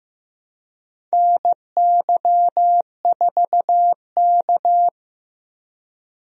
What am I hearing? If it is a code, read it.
Morse code, heard as a keyed tone, about 15 wpm, 700 Hz. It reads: NY4K
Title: USA Call Signs - 4 Characters - Repeated 3x - 15wpm